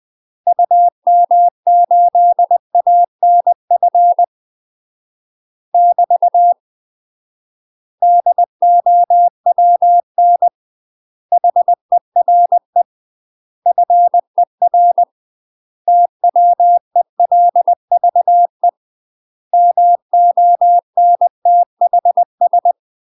Morse code, heard as a keyed tone, about 20 wpm, 700 Hz. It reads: UM8ANF <BT> DOWN HERE FER TWELVE MONTHS
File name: UK MASTER QSOs TRAINING/20WPM